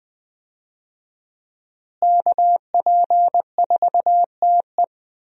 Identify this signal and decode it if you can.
Morse code, heard as a keyed tone, about 21 wpm, 700 Hz. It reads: KP4TE
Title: DXCC-100 Call Signs - 5 Characters - Repeated 2x - 20wpm